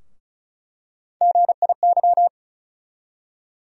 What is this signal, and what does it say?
Morse code, heard as a keyed tone, about 36 wpm, 700 Hz. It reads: GIY
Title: USA Call Sign Postfixes - 3 Letters 35wpm